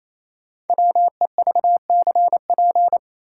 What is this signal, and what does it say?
Morse code, heard as a keyed tone, about 28 wpm, 700 Hz. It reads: WEVCP
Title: Code Groups - 5 Letter:Numbers - 28wpm